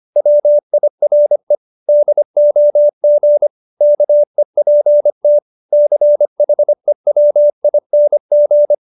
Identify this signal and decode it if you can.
Morse code, heard as a keyed tone, about 25 wpm, 590 Hz. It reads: WIRE DOG KEPT CHEWING